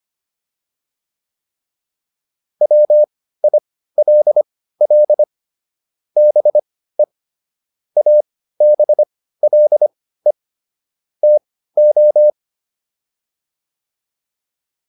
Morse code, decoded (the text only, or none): WILL BE ABLE TO